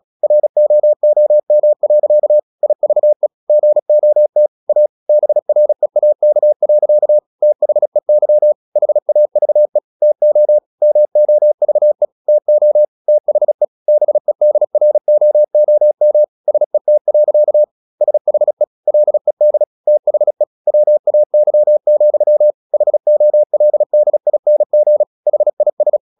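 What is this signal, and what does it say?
Morse code, heard as a keyed tone, about 37 wpm, 600 Hz. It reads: ROOM. IVE GOT A BREAK. THEY HAVE TO MOVE TO THE BEDROOM SET. SHE LED THE WAY, HOLDING HIS
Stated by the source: Morse practice at 36 wpm, text of Ken Follett's novel Never, from CW med Gnister.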